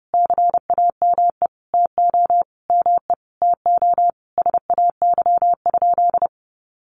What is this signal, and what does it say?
Morse code, heard as a keyed tone, about 30 wpm, 700 Hz. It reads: CAME TO ME TO SAY?
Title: Sentences from Top 200 Words 30wpm